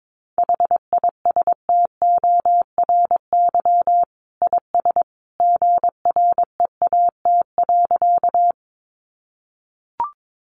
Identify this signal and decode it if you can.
Morse code, heard as a keyed tone, about 22 wpm, 700 Hz. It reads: HISTORY IS GREAT.
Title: Sentences from Top 600 Words 22wpm